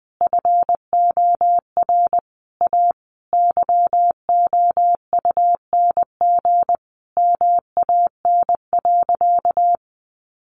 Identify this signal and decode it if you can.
Morse code, heard as a keyed tone, about 20 wpm, 700 Hz. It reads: FOR A YOUNG MAN.
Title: Sentences from Top 800 Words 20wpm